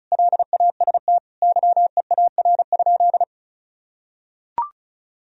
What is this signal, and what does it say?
Morse code, heard as a keyed tone, about 34 wpm, 700 Hz. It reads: LAST YEAR?
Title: Sentences from Top 1000 Words - Review Part 4 - 35wpm